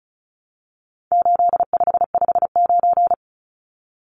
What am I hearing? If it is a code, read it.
Morse code, heard as a keyed tone, about 34 wpm, 700 Hz. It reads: 8559
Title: Numbers - 4-Digit 35wpm